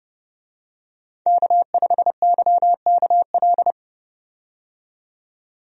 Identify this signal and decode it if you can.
Morse code, heard as a keyed tone, about 30 wpm, 700 Hz. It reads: K5YKL
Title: USA Call Signs - 5 Characters - 30wpm